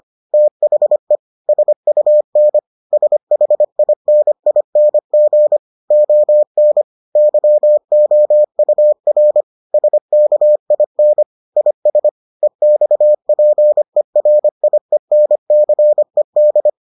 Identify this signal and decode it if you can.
Morse code, heard as a keyed tone, about 25 wpm, 600 Hz. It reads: THE SUN SHINING ON YOUR SKIN IS EXPERIENCED